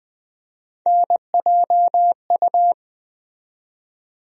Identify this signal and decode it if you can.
Morse code, heard as a keyed tone, about 20 wpm, 700 Hz. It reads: NJU
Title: USA Call Sign Postfixes - 3 Letters 20wpm